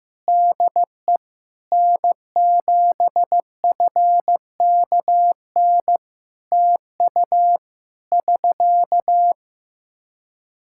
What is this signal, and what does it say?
Morse code, heard as a keyed tone, about 15 wpm, 700 Hz. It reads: DE N7FKN TU <SK>
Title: Practice QSO - Exchange 4 - Ending - 15wpm